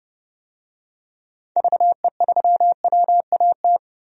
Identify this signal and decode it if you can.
Morse code, heard as a keyed tone, about 31 wpm, 700 Hz. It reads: VE3WAT
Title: DXCC-100 Call Signs - 6 Characters - 30wpm